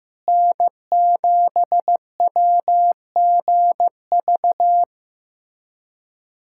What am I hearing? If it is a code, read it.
Morse code, heard as a keyed tone, about 15 wpm, 700 Hz. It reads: N7WGV
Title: USA Callsign DE Callsign - 5 Characters - 15wpm